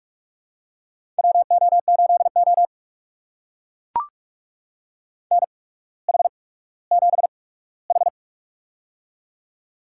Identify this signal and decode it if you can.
Morse code, heard as a keyed tone, about 43 wpm, spaced as for 19 wpm, 700 Hz. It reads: WO8O NH7H
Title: USA Call Signs - 4 Characters - 8x Character Spacing - 45wpm